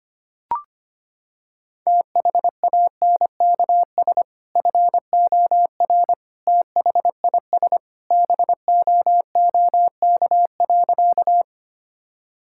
Morse code, heard as a keyed tone, about 25 wpm, 700 Hz. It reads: THANKS FOR THIS BOOK.
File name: practice / 25wpm / Sentences with 2-Word Phrases - 25wpm